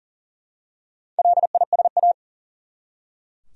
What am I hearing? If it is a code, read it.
Morse code, heard as a keyed tone, about 40 wpm, 700 Hz. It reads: LISA
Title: USA Common Names - Max 5 Letters - 40wpm